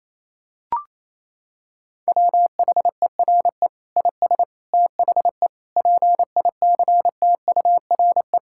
Morse code, heard as a keyed tone, about 28 wpm, 700 Hz. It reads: WHERE IS THE PICTURE
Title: Sentences from Top 1000 Words - Review Part 6 - 28wpm